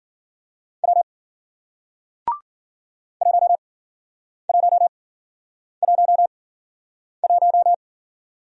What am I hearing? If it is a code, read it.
Morse code, heard as a keyed tone, about 52 wpm, spaced as for 9 wpm, 700 Hz. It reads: W 1111